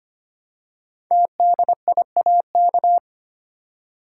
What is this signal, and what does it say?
Morse code, heard as a keyed tone, about 25 wpm, 700 Hz. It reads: TDIAK
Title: Code Groups - 5 Letters - 25wpm